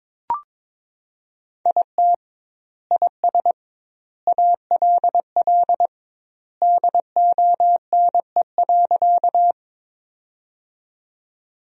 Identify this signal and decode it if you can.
Morse code, heard as a keyed tone, about 22 wpm, 700 Hz. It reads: IT IS ALL DONE.
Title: 2x Word-Spacing - Sentences from Top 500 Words - 22wpm